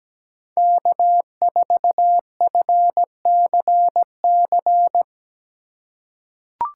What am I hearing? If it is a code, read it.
Morse code, heard as a keyed tone, about 17 wpm, 700 Hz. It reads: K4FCC
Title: K1USN SST QSO Practice - 17wpm